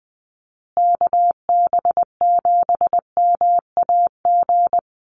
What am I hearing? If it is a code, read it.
Morse code, heard as a keyed tone, about 20 wpm, 700 Hz. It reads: KB7MAG
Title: USA Call Signs Repeated 3x 20wpm